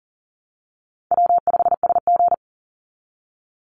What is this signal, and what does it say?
Morse code, heard as a keyed tone, about 40 wpm, 700 Hz. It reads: W5SG
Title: Call Signs Repeated 2x - 40wpm